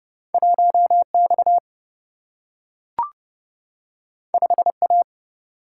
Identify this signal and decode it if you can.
Morse code, heard as a keyed tone, about 30 wpm, 700 Hz. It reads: 1 X 5 A
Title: Single-Letter Single-Number 30wpm